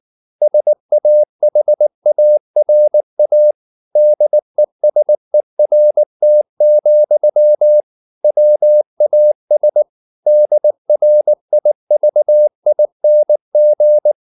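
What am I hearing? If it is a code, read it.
Morse code, heard as a keyed tone, about 19 wpm, 600 Hz. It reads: SAHARA DESERT, WAS DRIVING
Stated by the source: Morse code at 19 wpm from CW med Gnister, reading Ken Follett's novel Never in English.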